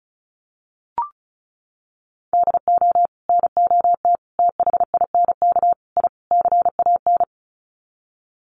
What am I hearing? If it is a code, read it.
Morse code, heard as a keyed tone, about 34 wpm, 700 Hz. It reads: DO NOT THINK I CAN
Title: Sets of 5 Words Encore2 35wpm